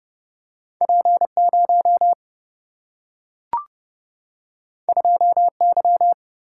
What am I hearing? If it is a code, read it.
Morse code, heard as a keyed tone, about 30 wpm, 700 Hz. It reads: P 0 2 Y
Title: Single-Letter Single-Number 30wpm